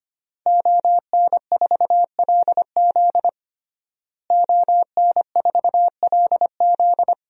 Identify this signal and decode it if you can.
Morse code, heard as a keyed tone, about 25 wpm, 700 Hz. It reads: ON4LZ ON4LZ
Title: DXCC-100 Call Signs - 5 Characters - Repeated 3x - 25wpm